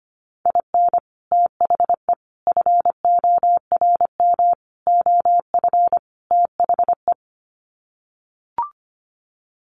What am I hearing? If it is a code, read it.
Morse code, heard as a keyed tone, about 25 wpm, 700 Hz. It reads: IN THE FORM OF THE
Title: Sets of 5 Words Encore2 25wpm